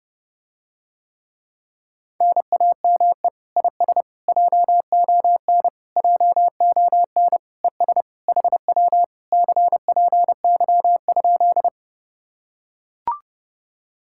Morse code, heard as a keyed tone, about 30 wpm, 700 Hz. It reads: NAME IS JON JON ES HW CPY?